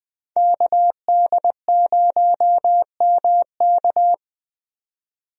Morse code, heard as a keyed tone, about 20 wpm, 700 Hz. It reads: KD0MK
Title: USA Call Signs - 5 Characters - 20wpm